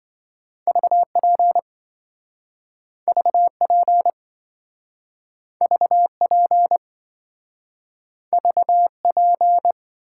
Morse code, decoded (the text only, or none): VP VP VP VP